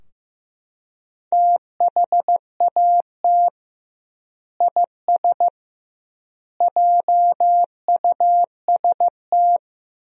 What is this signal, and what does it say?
Morse code, heard as a keyed tone, about 15 wpm, 700 Hz. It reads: THAT IS JUST
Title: 2x Word-Spacing - Sentences from Top 200 Words - 15wpm